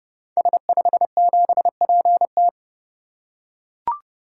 Morse code, heard as a keyed tone, about 30 wpm, 700 Hz. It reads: S57PT
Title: Code Groups - 5 Letter:Numbers - 30wpm